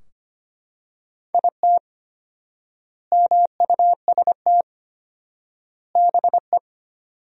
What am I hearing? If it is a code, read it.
Morse code, heard as a keyed tone, about 25 wpm, 700 Hz. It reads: IT MUST BE
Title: 4x Word-Spacing - Sentences from Top 500 Words - 25wpm